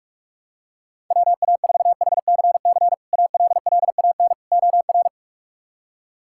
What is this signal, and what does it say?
Morse code, heard as a keyed tone, about 46 wpm, 700 Hz. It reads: WA4HCC ALLAN OR
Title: K1USN SST QSO Practice - 45wpm